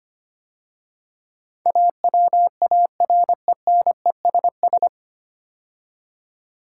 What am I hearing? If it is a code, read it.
Morse code, heard as a keyed tone, about 25 wpm, 690 Hz. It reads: AWARENESS
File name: practice / 25wpm / 9 Letter Words - First 3 Letters Given - 25wpm